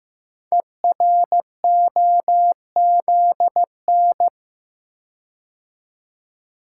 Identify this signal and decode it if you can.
Morse code, heard as a keyed tone, about 15 wpm, 700 Hz. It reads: EROZN
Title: Code Groups - 5 Letters - 15wpm